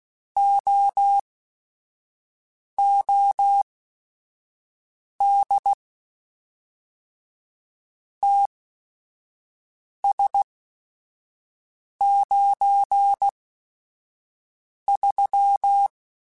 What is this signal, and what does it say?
Morse code, heard as a keyed tone, about 16 wpm, spaced as for 5 wpm, 800 Hz. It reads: OOD TS93